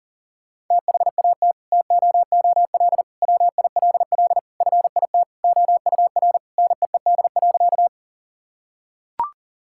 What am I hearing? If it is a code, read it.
Morse code, heard as a keyed tone, about 40 wpm, 700 Hz. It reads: THAT TOOL WILL FIT OUR NEED.
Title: Sentences from Top 1000 Words - Review Part 6 - 40wpm